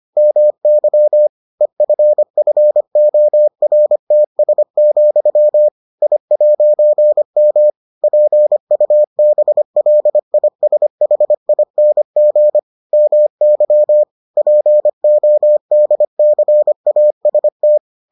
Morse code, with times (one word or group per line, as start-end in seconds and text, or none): 0.2-1.3 MY
1.6-5.7 EFFORTS,
6.0-7.7 I'M
8.0-12.6 PUBLISHING
12.9-14.0 MY
14.4-17.8 PODCAST